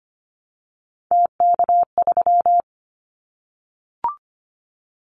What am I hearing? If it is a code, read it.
Morse code, heard as a keyed tone, about 25 wpm, 700 Hz. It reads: TK3
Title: Callsign Prefixes for Top 100 DXCC - 25wpm